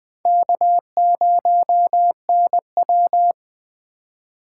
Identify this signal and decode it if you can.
Morse code, heard as a keyed tone, about 20 wpm, 700 Hz. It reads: K0NW
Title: K1USN SST QSO Practice - 20wpm